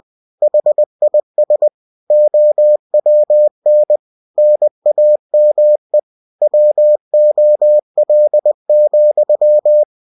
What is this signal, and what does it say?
Morse code, heard as a keyed tone, about 20 wpm, 600 Hz. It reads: HIS OWN NAME WOL,
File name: cw-books / winnie-the-pooh / winnie-20wpm-page-0029